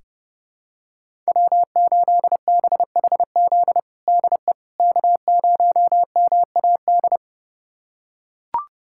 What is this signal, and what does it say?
Morse code, heard as a keyed tone, about 32 wpm, 700 Hz. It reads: W8BHZ DE K0MAD
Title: USA Callsign DE Callsign - 5 Characters - 30wpm